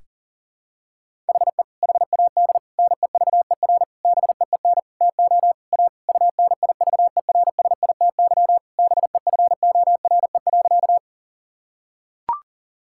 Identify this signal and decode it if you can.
Morse code, heard as a keyed tone, about 40 wpm, 700 Hz. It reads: HE HAD NEVER BEEN TO A UNIVERSITY BEFORE.
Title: Sentences from Top 1000 Words - Review Part 5 - 40wpm